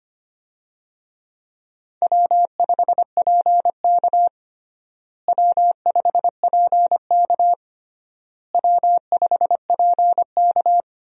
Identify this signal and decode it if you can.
Morse code, heard as a keyed tone, about 25 wpm, 700 Hz. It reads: W5PK W5PK W5PK
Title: USA Call Signs - 4 Characters - Repeated 3x - 25wpm